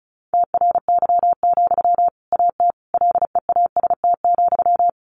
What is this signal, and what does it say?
Morse code, heard as a keyed tone, about 34 wpm, 700 Hz. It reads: TRY, AT LEAST,
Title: Sentences from Top 700 Words 35wpm